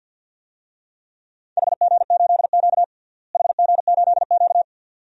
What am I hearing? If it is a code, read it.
Morse code, heard as a keyed tone, about 51 wpm, 700 Hz. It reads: HG8Q HG8Q
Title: DXCC-100 Call Signs - 4 Characters - Repeated 2x - 50wpm